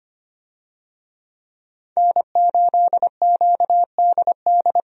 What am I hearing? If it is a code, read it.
Morse code, heard as a keyed tone, about 25 wpm, 700 Hz. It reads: N8QDD
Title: USA Call Signs - 5 Characters - 25wpm